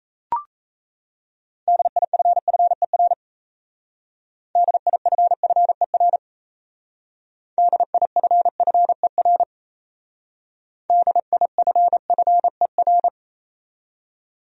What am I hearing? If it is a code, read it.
Morse code, heard as a keyed tone, about 35 wpm, 700 Hz. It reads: DIFFER DIFFER DIFFER DIFFER